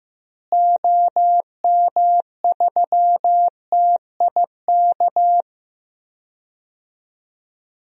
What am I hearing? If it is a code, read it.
Morse code, heard as a keyed tone, about 15 wpm, 700 Hz. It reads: OM3TIK
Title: DXCC-100 Call Signs - 6 Characters - Repeated 3x - 15wpm